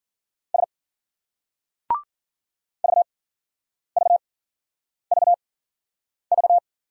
Morse code, heard as a keyed tone, about 56 wpm, spaced as for 9 wpm, 700 Hz. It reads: S VVVV